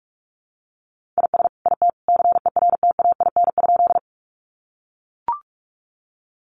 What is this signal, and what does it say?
Morse code, heard as a keyed tone, about 44 wpm, 700 Hz. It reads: IS IT CERTAIN?
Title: Sentences from Top 500 Words 45wpm